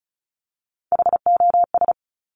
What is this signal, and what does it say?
Morse code, heard as a keyed tone, about 36 wpm, 700 Hz. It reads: HOS
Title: Top Three Letter Combinations in Words 35wpm